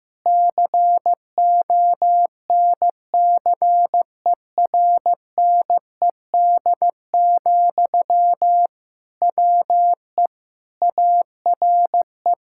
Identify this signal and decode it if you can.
Morse code, heard as a keyed tone, about 15 wpm, 700 Hz. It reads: CONCERNED, WE ARE
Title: Sentences from Top 1000 Words - Review Part 6 - 15wpm